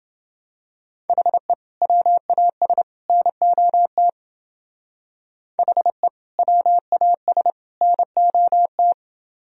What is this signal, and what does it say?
Morse code, heard as a keyed tone, about 29 wpm, 700 Hz. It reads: HE WAS NOT HE WAS NOT